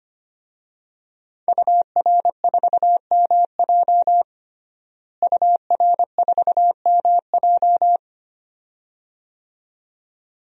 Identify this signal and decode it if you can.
Morse code, heard as a keyed tone, about 25 wpm, 700 Hz. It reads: UR4MJ UR4MJ
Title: DXCC-100 Call Signs - 5 Characters - Repeated 2x - 25wpm